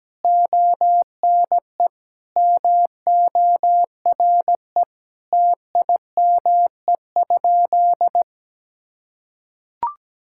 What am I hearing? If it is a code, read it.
Morse code, heard as a keyed tone, about 17 wpm, 700 Hz. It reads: ONE MORE TIME?
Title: Sentences from Top 1000 Words - Review Part 5 - 17wpm